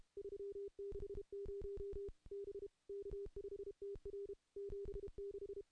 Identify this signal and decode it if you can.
Morse code, heard as a keyed tone, about 32 wpm, 400 Hz. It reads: 360 B K5TR 76
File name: contests / CW-SS-k5tr- -0000